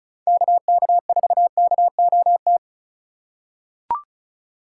Morse code, heard as a keyed tone, about 36 wpm, 700 Hz. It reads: KK4KOT
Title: USA Call Signs - 2x Character Spacing - 35wpm